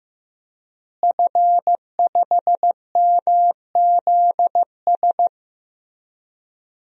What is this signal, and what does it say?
Morse code, heard as a keyed tone, about 15 wpm, 700 Hz. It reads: F5MZS
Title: DXCC-100 Call Signs - 5 Characters - Repeated 3x - 15wpm